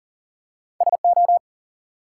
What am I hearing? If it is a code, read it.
Morse code, heard as a keyed tone, about 44 wpm, 700 Hz. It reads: SO
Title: Two Letter Words - 40wpm